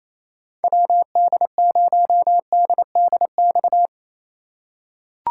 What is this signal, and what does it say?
Morse code, heard as a keyed tone, about 28 wpm, 700 Hz. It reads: WD0DDX
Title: USA Call Signs - 6 Characters - 4x Character Spacing - 28wpm